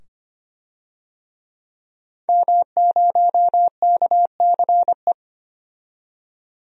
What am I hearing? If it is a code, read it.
Morse code, heard as a keyed tone, about 25 wpm, 700 Hz. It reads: M0KCE
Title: DXCC-100 Call Signs - 5 Characters - 25wpm